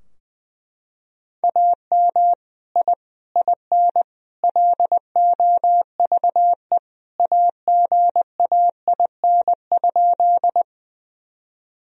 Morse code, heard as a keyed tone, about 20 wpm, 700 Hz. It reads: AM I IN LOVE AGAIN?
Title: Sentences from Top 1000 Words - Review Part 2 - 20wpm